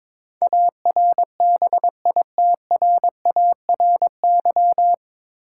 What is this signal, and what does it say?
Morse code, heard as a keyed tone, about 22 wpm, 700 Hz. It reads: ARBITRARY
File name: 9 Letter Words - First 3 Letters Given - 22wpm